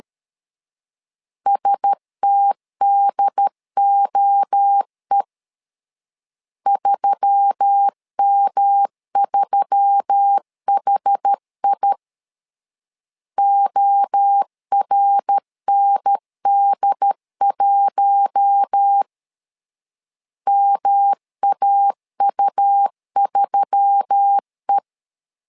Morse code, heard as a keyed tone, about 13 wpm, 800 Hz. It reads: STDOE 3M3HI ORND1 MAU3E
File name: LEZIONI / 4 Q1S3 12wpm 2023